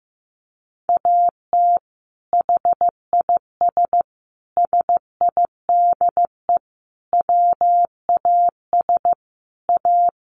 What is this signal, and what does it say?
Morse code, heard as a keyed tone, about 15 wpm, 700 Hz. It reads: AT HIS SIDE WAS A
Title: Sentences from Top 500 Words 15wpm